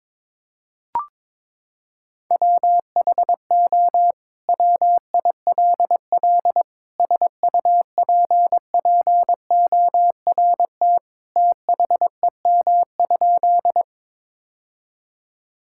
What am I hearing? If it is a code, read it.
Morse code, heard as a keyed tone, about 22 wpm, 700 Hz. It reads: WHO WILL SUPPORT THEM?
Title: Sentences from Top 1000 Words - Review Part 5 - 22wpm